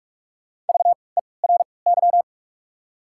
Morse code, heard as a keyed tone, about 44 wpm, spaced as for 25 wpm, 700 Hz. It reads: VERY
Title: 3x Character Spacing - Top 500 Words - 45wpm